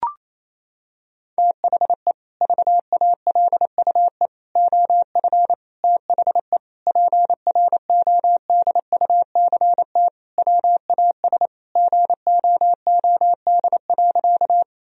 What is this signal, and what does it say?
Morse code, heard as a keyed tone, about 28 wpm, 700 Hz. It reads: THE VALUE OF THE PRODUCT WAS GOOD.
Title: Sentences from Top 1000 Words - Review Part 3 - 28wpm